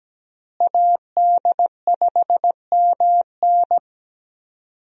Morse code, heard as a keyed tone, about 17 wpm, 700 Hz. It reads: AD5MN
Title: USA Call Signs - 5 Characters - Repeated 3x - 17wpm